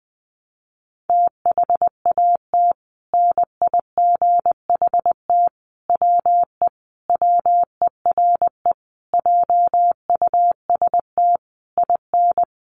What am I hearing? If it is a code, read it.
Morse code, heard as a keyed tone, about 20 wpm, 700 Hz. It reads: THAT NIGHT WE WERE JUST IN
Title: Sentences from Top 300 Words 20wpm